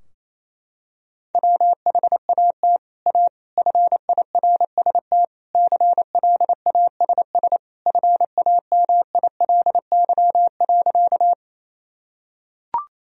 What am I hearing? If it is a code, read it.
Morse code, heard as a keyed tone, about 28 wpm, 700 Hz. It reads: WHAT A FIRST CLASS FAMILY.
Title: Sentences from Top 1000 Words - Review Part 2 - 28wpm